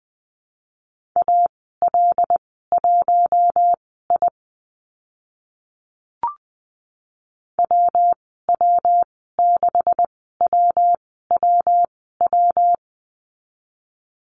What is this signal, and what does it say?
Morse code, heard as a keyed tone, about 20 wpm, spaced as for 14 wpm, 700 Hz. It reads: AL1I WW6WWW
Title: 140 Lesson 23g - Letter f - Call Signs 20wpm